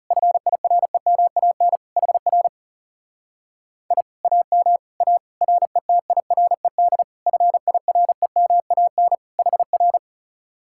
Morse code, heard as a keyed tone, about 37 wpm, 700 Hz. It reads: FIREMAN HR I AM A RETIRED FIREMAN HR
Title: Speed-Racing Common QSO Phrases - 30wpm